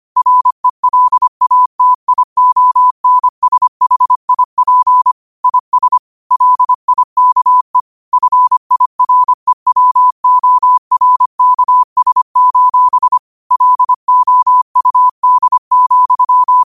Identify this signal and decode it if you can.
Morse code, heard as a keyed tone, about 25 wpm, 1000 Hz. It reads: RELATIONSHIP IS LIKE FIREWORKS: LOUD,